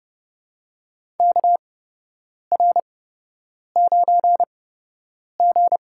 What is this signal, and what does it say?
Morse code, heard as a keyed tone, about 30 wpm, spaced as for 8 wpm, 700 Hz. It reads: KR9G
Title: USA Call Signs - 8x Character Spacing - 30wpm